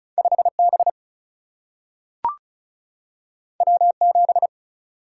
Morse code, heard as a keyed tone, about 36 wpm, 700 Hz. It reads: 5B W7
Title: Single-Letter Single-Number 35wpm